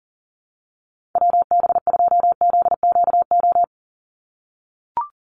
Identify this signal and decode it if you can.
Morse code, heard as a keyed tone, about 40 wpm, 700 Hz. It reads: WB2ZQO
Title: USA Call Signs 40wpm